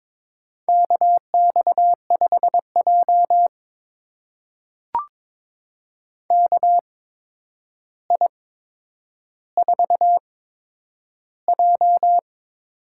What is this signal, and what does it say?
Morse code, heard as a keyed tone, about 22 wpm, spaced as for 9 wpm, 700 Hz. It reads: KX5J KI4J
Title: USA Call Signs - 4 Characters - 8x Character Spacing - 22wpm